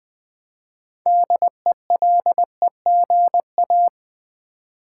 Morse code, heard as a keyed tone, about 20 wpm, 700 Hz. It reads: DELEGA
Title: Build A Word - 2 Letters At A Time - 8-Letter Words - 20wpm